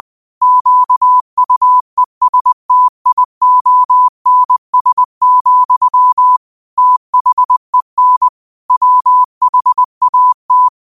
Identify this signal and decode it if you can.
Morse code, heard as a keyed tone, about 20 wpm, 1000 Hz. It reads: QUESTIONS, THEN WHAT